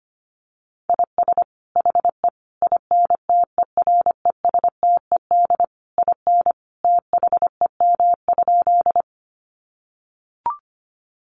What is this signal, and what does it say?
Morse code, heard as a keyed tone, about 25 wpm, 700 Hz. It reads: IS HE INTERESTED IN THEM?
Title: Sentences from Top 900 Words 25wpm